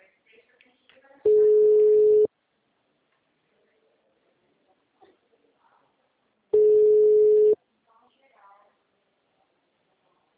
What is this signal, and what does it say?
Ringback tone (Europe, 425 Hz), with speech.